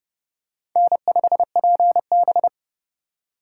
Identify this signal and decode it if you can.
Morse code, heard as a keyed tone, about 28 wpm, 700 Hz. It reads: N5PB